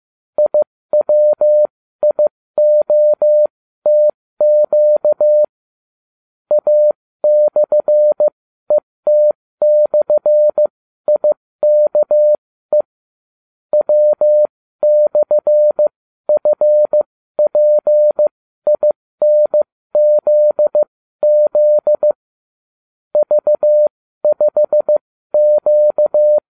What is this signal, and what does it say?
Morse code, heard as a keyed tone, about 15 wpm, spaced as for 13 wpm, 600 Hz. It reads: IWIOTQ A/ET/IKE W/FPINZZ V5Q